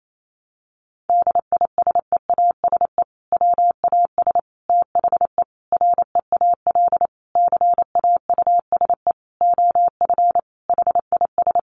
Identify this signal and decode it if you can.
Morse code, heard as a keyed tone, about 28 wpm, 700 Hz. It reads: DISEASE WAS THE REAL CAUSE OF HIS
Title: Sentences from Top 1100 Words 28wpm